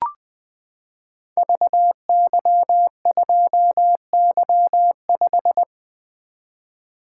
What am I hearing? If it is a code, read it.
Morse code, heard as a keyed tone, about 20 wpm, 700 Hz. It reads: VY2Y5
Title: Code Groups - 5 Letter:Numbers - 20wpm